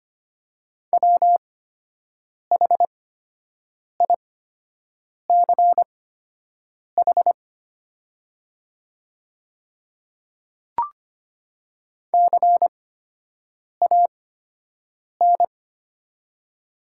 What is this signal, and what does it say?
Morse code, heard as a keyed tone, about 25 wpm, spaced as for 7 wpm, 700 Hz. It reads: WHICH CAN
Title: 8x Character Spacing - Top 100 Words - 25wpm